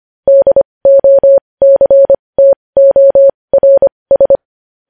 Morse code, heard as a keyed tone, about 25 wpm, spaced as for 20 wpm, 550 Hz. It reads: DOCTORS
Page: News Headlines in Morse Code at 20 WPM